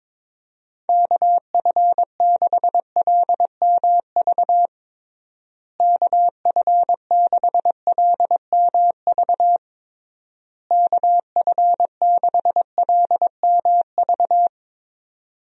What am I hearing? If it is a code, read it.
Morse code, heard as a keyed tone, about 22 wpm, 700 Hz. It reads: KF6LMV KF6LMV KF6LMV
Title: USA Call Signs - 6 Characters - Repeated 3x - 22wpm